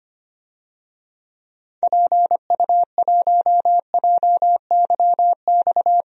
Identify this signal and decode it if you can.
Morse code, heard as a keyed tone, about 25 wpm, 700 Hz. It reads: PU1JYX
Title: DXCC-100 Call Signs - 6 Characters - 25wpm